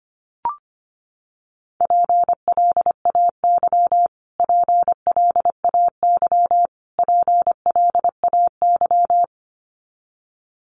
Morse code, heard as a keyed tone, about 25 wpm, 700 Hz. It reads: PLAY PLAY PLAY
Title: Top 200 Words Repeated 25wpm